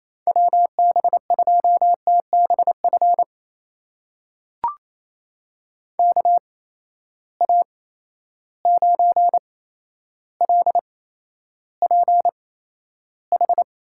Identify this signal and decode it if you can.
Morse code, heard as a keyed tone, about 28 wpm, spaced as for 12 wpm, 700 Hz. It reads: WB2TBF KA9LPH
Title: USA Call Signs - 6 Characters - 8x Character Spacing - 28wpm